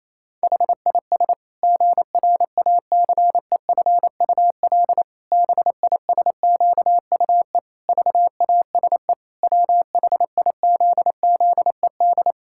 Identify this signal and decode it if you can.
Morse code, heard as a keyed tone, about 28 wpm, 700 Hz. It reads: HIS GRACEFUL BISQUE VASE WHIZZED
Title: Pangram Challenge v2 - 28wpm